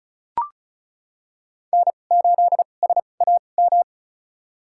Morse code, heard as a keyed tone, about 34 wpm, spaced as for 25 wpm, 700 Hz. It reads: N8SAM
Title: USA Call Signs - 5 Characters - 2x Character Spacing - 35wpm